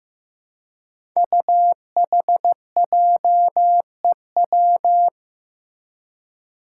Morse code, heard as a keyed tone, about 15 wpm, 700 Hz. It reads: UHJEW
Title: Five Challenging Letter-Number Combinations 15wpm